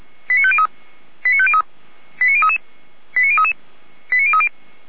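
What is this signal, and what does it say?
Five-tone selective call (ZVEI), heard as quick runs of five tones, tones of about 70 ms. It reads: 89503 89503 89033 89033 89030
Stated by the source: Received on SDR as ZVEI five-tone selective call; archived by feuerwehr-orlamuende.org (fire brigade Orlamünde).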